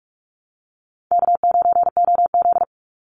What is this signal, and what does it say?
Morse code, heard as a keyed tone, about 44 wpm, 700 Hz. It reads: K9OZ
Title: Easy Call Signs Repeated 3x - 45wpm